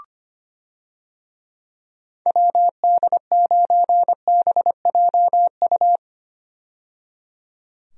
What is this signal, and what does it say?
Morse code, heard as a keyed tone, about 25 wpm, 700 Hz. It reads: WD9BJU
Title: USA Call Signs - 6 Characters - 25wpm